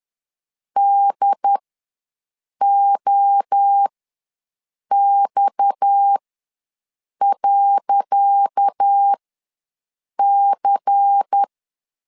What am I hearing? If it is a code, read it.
Morse code, heard as a keyed tone, about 10 wpm, spaced as for 6 wpm, 800 Hz. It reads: DOX.C
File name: LEZIONI / 10 SIGNS / 10C DOT ws 10wpm 4213